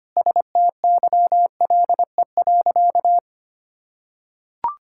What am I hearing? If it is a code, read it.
Morse code, heard as a keyed tone, about 25 wpm, 700 Hz. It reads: STYLE.
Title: Sentences from Top 1000 Words - Review Part 5 - 25wpm